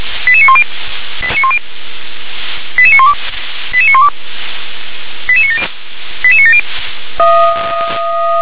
Five-tone selective call (ZVEI), heard as quick runs of five tones, tones of about 70 ms. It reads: 88910 88912 88912 88970 88970
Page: SDR/ZVEI